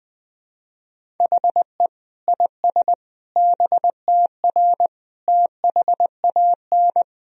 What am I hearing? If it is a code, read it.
Morse code, heard as a keyed tone, about 21 wpm, 700 Hz. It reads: HE IS BTR THAN
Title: Sentences from Top 400 Words and Abbreviations 20wpm